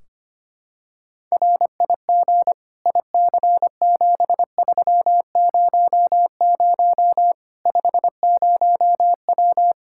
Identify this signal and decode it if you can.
Morse code, heard as a keyed tone, about 25 wpm, 700 Hz. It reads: RIG IC7300 50W
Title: Practice QSO - Exchange 2 - Info - 25wpm